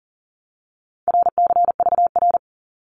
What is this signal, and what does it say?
Morse code, heard as a keyed tone, about 42 wpm, 700 Hz. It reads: RCVR
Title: Abbreviations 40wpm